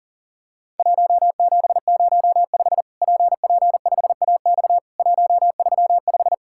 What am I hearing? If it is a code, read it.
Morse code, heard as a keyed tone, about 43 wpm, 700 Hz. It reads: 1705 PP5AX 135
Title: Worked All Europe DX Contest - QTC Exchange - 40wpm